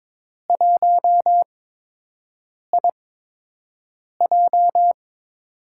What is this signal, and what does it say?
Morse code, heard as a keyed tone, about 22 wpm, spaced as for 6 wpm, 700 Hz. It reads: 1IJ
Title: USA Call Signs - 4 Characters - 8x Character Spacing - 22wpm